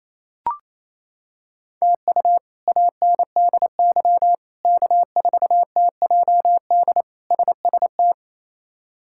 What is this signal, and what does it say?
Morse code, heard as a keyed tone, about 28 wpm, 700 Hz. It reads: TU ANDY K4TJD SST
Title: K1USN SST QSO Practice - 28wpm